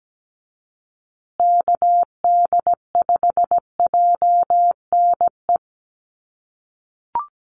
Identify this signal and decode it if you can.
Morse code, heard as a keyed tone, about 17 wpm, 700 Hz. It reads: KD5JNE
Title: USA Call Signs Repeated 3x 17wpm